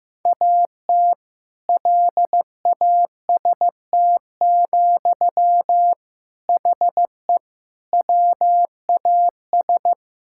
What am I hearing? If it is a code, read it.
Morse code, heard as a keyed tone, about 15 wpm, 700 Hz. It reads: AT LAST, HE WAS
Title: Sentences from Top 1000 Words - Review Part 6 - 15wpm